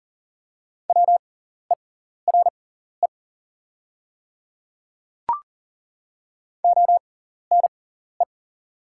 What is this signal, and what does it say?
Morse code, heard as a keyed tone, about 40 wpm, spaced as for 14 wpm, 690 Hz. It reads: WERE ONE
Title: 6x Character Spacing - Top 300 Words - 40wpm